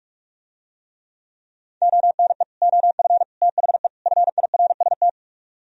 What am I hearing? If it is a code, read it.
Morse code, heard as a keyed tone, about 44 wpm, 700 Hz. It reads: ONE OF THE FIRST